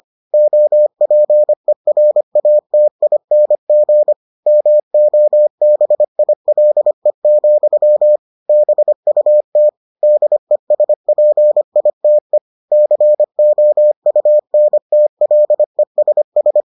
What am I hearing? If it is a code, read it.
Morse code, heard as a keyed tone, about 25 wpm, 600 Hz. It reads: OPERATING MOBILE, BUT DESPITE COUNTLESS